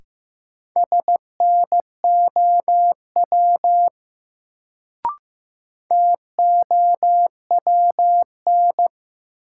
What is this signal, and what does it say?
Morse code, heard as a keyed tone, about 15 wpm, 700 Hz. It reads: SNOW TOWN
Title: Sentences from Top 500 Words - New Word Drill - 15wpm